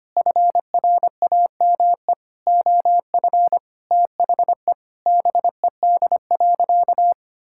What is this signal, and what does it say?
Morse code, heard as a keyed tone, about 25 wpm, 700 Hz. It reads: FRAME OF THE BED.
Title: Sentences from Top 1000 Words - Review Part 6 - 25wpm